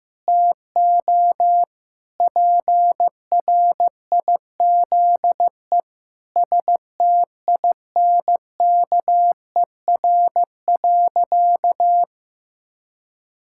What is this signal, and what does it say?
Morse code, heard as a keyed tone, about 16 wpm, 700 Hz. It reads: TO PRIZE STINKER.
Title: Pangram Challenge v2 - 15wpm